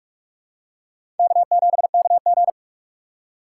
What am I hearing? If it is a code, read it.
Morse code, heard as a keyed tone, about 46 wpm, 700 Hz. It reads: K7KG
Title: USA Call Signs - 6x Character Spacing - 45wpm